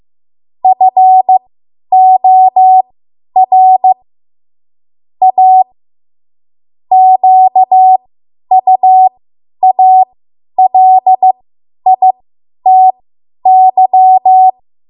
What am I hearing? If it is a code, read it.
Morse code, heard as a keyed tone, about 15 wpm, spaced as for 10 wpm, 750 Hz. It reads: FOR A QUALITY
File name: Morse / 10WPM